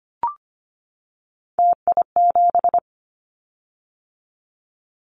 Morse code, heard as a keyed tone, about 25 wpm, 700 Hz. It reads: TI7
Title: Callsign Prefixes for Top 100 DXCC - 25wpm